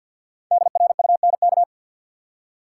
Morse code, heard as a keyed tone, about 47 wpm, 700 Hz. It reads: DRUNK
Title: Five Letter Words - 50wpm